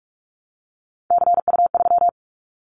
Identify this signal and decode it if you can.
Morse code, heard as a keyed tone, about 46 wpm, 700 Hz. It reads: CU3
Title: Callsign Prefixes for Top 100 DXCC - 45wpm